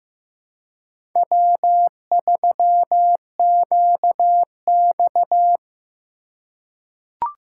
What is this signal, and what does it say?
Morse code, heard as a keyed tone, about 15 wpm, 700 Hz. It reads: W3QX
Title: USA Call Signs - 4 Characters - 3x Character Spacing - 15wpm